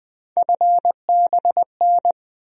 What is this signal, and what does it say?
Morse code, heard as a keyed tone, about 20 wpm, 700 Hz. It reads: FBN
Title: USA Call Sign Postfixes - 3 Letters 20wpm